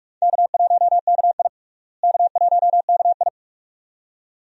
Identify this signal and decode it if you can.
Morse code, heard as a keyed tone, about 47 wpm, 700 Hz. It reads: K1KI K1KI
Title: USA Call Signs - 4 Characters - Repeated 2x - 45wpm